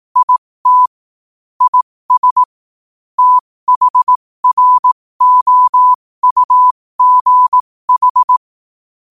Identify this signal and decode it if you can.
Morse code, heard as a keyed tone, about 18 wpm, spaced as for 15 wpm, 1000 Hz. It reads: IT IS THROUGH
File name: qotdfiles / 15wpm